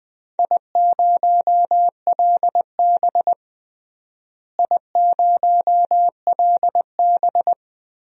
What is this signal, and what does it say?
Morse code, heard as a keyed tone, about 20 wpm, 700 Hz. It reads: I0LB I0LB
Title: DXCC-100 Call Signs - 4 Characters - Repeated 2x - 20wpm